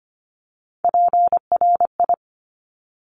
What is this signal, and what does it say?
Morse code, heard as a keyed tone, about 25 wpm, 700 Hz. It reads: PRI
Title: Top Three Letter Combinations in Words 25wpm